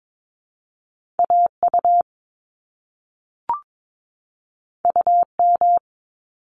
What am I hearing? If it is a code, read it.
Morse code, heard as a keyed tone, about 22 wpm, 700 Hz. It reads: AU UM
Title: Top Two Letter Combinations in Words 22wpm